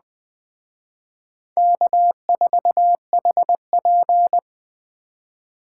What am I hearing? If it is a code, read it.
Morse code, heard as a keyed tone, about 20 wpm, 700 Hz. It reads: K4HP
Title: Speed Racing - USA 1x2 Call Signs 17wpm